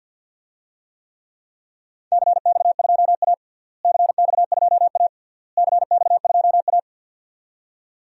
Morse code, heard as a keyed tone, about 48 wpm, 700 Hz. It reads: CX2A CX2A CX2A
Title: DXCC-100 Call Signs - 4 Characters - Repeated 3x - 50wpm